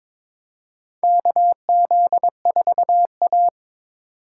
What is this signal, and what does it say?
Morse code, heard as a keyed tone, about 22 wpm, 700 Hz. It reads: KZ4A
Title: K1USN SST QSO Practice - 22wpm